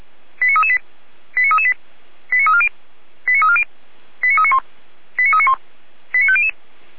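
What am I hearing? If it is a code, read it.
Five-tone selective call (ZVEI), heard as quick runs of five tones, tones of about 70 ms. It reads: 89338 89338 89340 89340 89391 89391 89500
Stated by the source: ZVEI five-tone call received by SDR, from Feuerwehr Orlamünde (fire brigade) archive.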